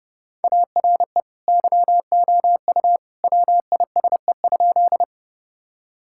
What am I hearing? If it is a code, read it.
Morse code, heard as a keyed tone, about 30 wpm, 700 Hz. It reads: ARE YOU WISE?
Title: Sentences from Top 1600 Words 30wpm